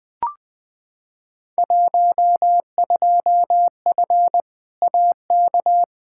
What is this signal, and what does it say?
Morse code, heard as a keyed tone, about 20 wpm, 700 Hz. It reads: 12F AK
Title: Field Day Exchanges Encore - 20wpm